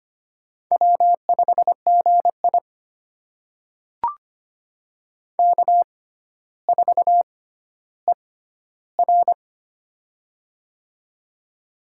Morse code, heard as a keyed tone, about 25 wpm, spaced as for 13 wpm, 700 Hz. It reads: W5GI K4ER
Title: USA Call Signs - 4 Characters - 6x Character Spacing - 25wpm